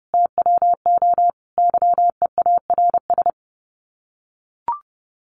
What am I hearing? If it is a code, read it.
Morse code, heard as a keyed tone, about 30 wpm, 700 Hz. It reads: TWO YEARS
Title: Sets of 5 Words Encore2 30wpm